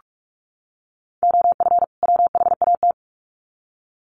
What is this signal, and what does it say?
Morse code, heard as a keyed tone, about 44 wpm, 700 Hz. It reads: OF WHAT
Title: Sets of 2 Words 45wpm